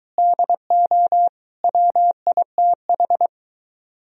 Morse code, heard as a keyed tone, about 23 wpm, 700 Hz. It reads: DO WITH